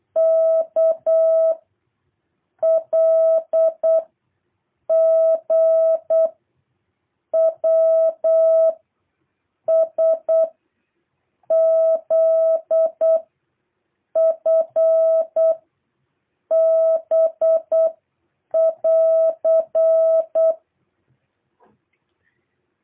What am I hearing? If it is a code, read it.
Morse code, heard as a keyed tone, about 8 wpm, 660 Hz. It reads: K L G W S Z F B <AR>
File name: jh8cbh / 2023cw